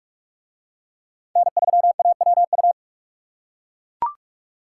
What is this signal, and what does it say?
Morse code, heard as a keyed tone, about 38 wpm, 700 Hz. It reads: N3AWU